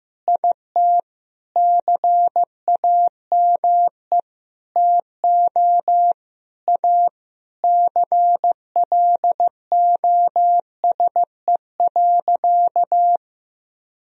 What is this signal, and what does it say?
Morse code, heard as a keyed tone, about 16 wpm, 700 Hz. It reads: IT CAME TO A CLOSE.
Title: Sentences from Top 1000 Words - Review Part 1 - 15wpm